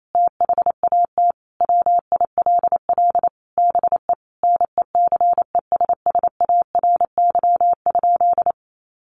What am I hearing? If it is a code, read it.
Morse code, heard as a keyed tone, about 28 wpm, 700 Hz. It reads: THAT WILL BE NECESSARY?
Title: Sentences from Top 800 Words 28wpm